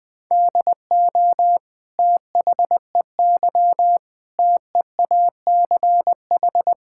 Morse code, heard as a keyed tone, about 20 wpm, 700 Hz. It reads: DO THEY TEACH
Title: Sentences from Top 1000 Words - Review Part 5 - 20wpm